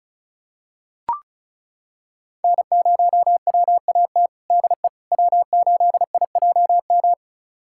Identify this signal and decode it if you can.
Morse code, heard as a keyed tone, about 36 wpm, 700 Hz. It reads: N0WAT DE W8IJM
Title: USA Callsign DE Callsign - 5 Characters - 35wpm